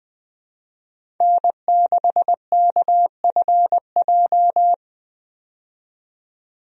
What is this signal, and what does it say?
Morse code, heard as a keyed tone, about 20 wpm, 700 Hz. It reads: N6KFJ
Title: USA Call Signs - 5 Characters - 20wpm